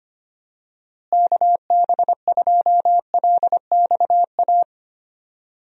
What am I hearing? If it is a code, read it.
Morse code, heard as a keyed tone, about 25 wpm, 700 Hz. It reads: KB2LXA
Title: USA Call Signs - 6 Characters - Repeated 3x - 25wpm